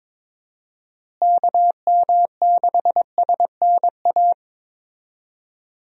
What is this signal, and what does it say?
Morse code, heard as a keyed tone, about 22 wpm, 700 Hz. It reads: KM6SNA
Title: USA Call Signs - 6 Characters - Repeated 3x - 22wpm